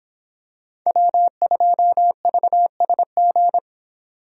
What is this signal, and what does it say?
Morse code, heard as a keyed tone, about 26 wpm, 700 Hz. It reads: W2VSG